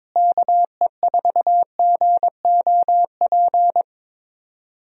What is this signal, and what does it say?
Morse code, heard as a keyed tone, about 22 wpm, 700 Hz. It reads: KE4GOP
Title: USA Call Signs - 6 Characters - Repeated 3x - 22wpm